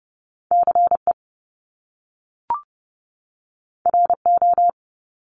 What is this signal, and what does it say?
Morse code, heard as a keyed tone, about 30 wpm, 700 Hz. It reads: CE RO
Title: Top Two Letter Combinations in Words 30wpm